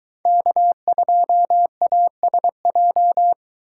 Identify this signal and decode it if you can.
Morse code, heard as a keyed tone, about 23 wpm, 700 Hz. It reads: K2ASJ